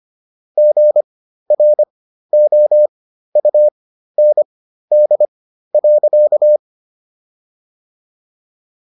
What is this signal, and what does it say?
Morse code, heard as a keyed tone, about 25 wpm, spaced as for 13 wpm, 600 Hz. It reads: GROUND.